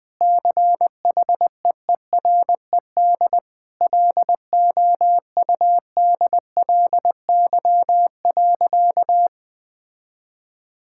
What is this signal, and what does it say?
Morse code, heard as a keyed tone, about 20 wpm, 700 Hz. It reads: CHEERED LOUDLY.
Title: Sentences with 2-Word Phrases - Spoken Beginning - 20wpm